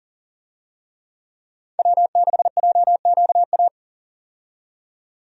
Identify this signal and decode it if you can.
Morse code, heard as a keyed tone, about 39 wpm, 700 Hz. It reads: W6JQA
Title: USA Call Signs - 5 Characters - 40wpm